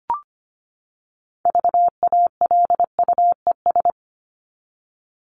Morse code, heard as a keyed tone, about 25 wpm, 700 Hz. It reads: VALUES
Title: ES:S Suffix - Top 100 Words with Base Words 25wpm